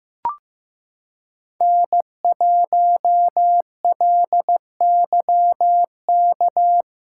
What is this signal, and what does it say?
Morse code, heard as a keyed tone, about 15 wpm, 700 Hz. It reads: N1LYK
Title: USA Callsign DE Callsign - 5 Characters - 15wpm